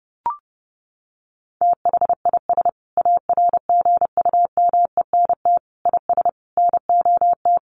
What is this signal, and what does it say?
Morse code, heard as a keyed tone, about 30 wpm, 700 Hz. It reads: THIS ARGUMENT IS NOT
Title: Sentences from Top 1400 Words 30wpm